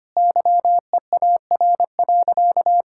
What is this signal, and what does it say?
Morse code, heard as a keyed tone, about 25 wpm, 700 Hz. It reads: YEAR.
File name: practice / 25wpm / Sentences with 4-Word Phrases - 25wpm